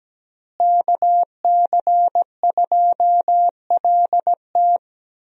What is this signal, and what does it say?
Morse code, heard as a keyed tone, about 17 wpm, 700 Hz. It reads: KC2LT
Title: USA Call Signs - 5 Characters - Repeated 2x - 17wpm